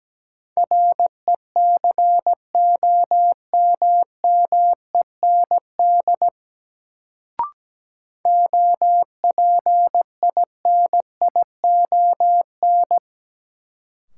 Morse code, heard as a keyed tone, about 17 wpm, 700 Hz. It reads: RECOMMEND OPINION
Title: Sentences from Top 1200 Words - New Word Drill Alt - 17wpm